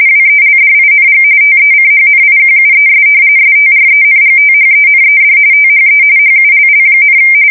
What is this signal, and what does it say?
RTTY, about 45.45 baud. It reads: WELDING TECHNIQUES STRENGTHENED THE HULL,
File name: fsk170 / usnavy4